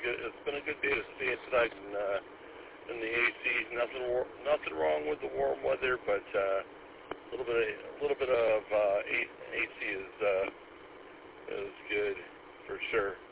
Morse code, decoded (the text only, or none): none